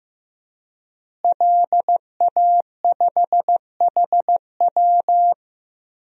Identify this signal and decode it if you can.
Morse code, heard as a keyed tone, about 15 wpm, 700 Hz. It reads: LA5HW
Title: DXCC-100 Call Signs - 5 Characters - Repeated 3x - 15wpm